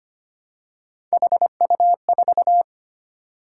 Morse code, heard as a keyed tone, about 25 wpm, 700 Hz. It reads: HU4
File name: Three Letter-Number Combinations - 25wpm